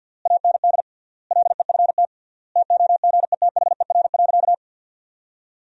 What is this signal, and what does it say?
Morse code, heard as a keyed tone, about 50 wpm, 700 Hz. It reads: AND LEFT TOGETHER.
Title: 3x Word-Spacing - Sentences from Top 400 Words - 50wpm